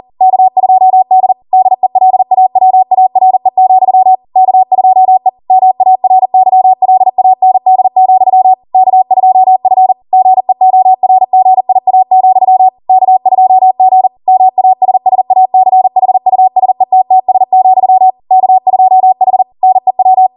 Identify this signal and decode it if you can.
Morse code, heard as a keyed tone, about 40 wpm, 750 Hz. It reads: K2D DELAWARE, K2E MARYLAND, K2F GEORGIA, K2G MASSACHUSETTS, K2H NEW